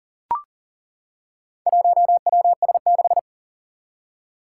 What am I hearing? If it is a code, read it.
Morse code, heard as a keyed tone, about 37 wpm, 700 Hz. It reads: 1WS6
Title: Four Challenging Letter-Number Combinations 40wpm